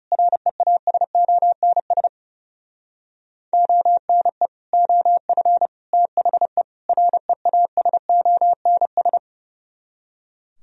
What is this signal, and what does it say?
Morse code, heard as a keyed tone, about 37 wpm, 700 Hz. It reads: REASONS ONE OF THE REASONS
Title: Speed-Racing - Sets of 4 Words 30wpm